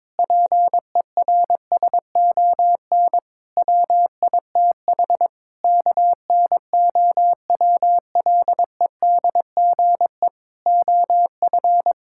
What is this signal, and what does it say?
Morse code, heard as a keyed tone, about 22 wpm, 700 Hz. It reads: PERSON WITH KNOWLEDGE OF
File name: Sets of 9 Words - 22wpm